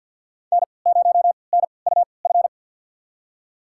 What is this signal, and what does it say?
Morse code, heard as a keyed tone, about 52 wpm, spaced as for 29 wpm, 700 Hz. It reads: N0NUF
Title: USA Call Signs - 5 Characters - 3x Character Spacing - 50wpm